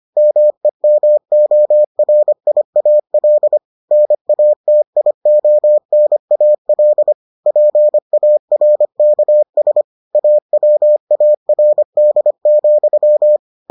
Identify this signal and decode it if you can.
Morse code, heard as a keyed tone, about 25 wpm, 600 Hz. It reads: MEMORIAL NATIONAL PARKS AWARD,